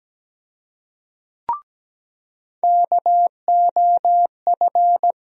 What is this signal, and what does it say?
Morse code, heard as a keyed tone, about 17 wpm, 700 Hz. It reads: KOF
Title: USA Call Sign Postfixes - 3 Letters 17wpm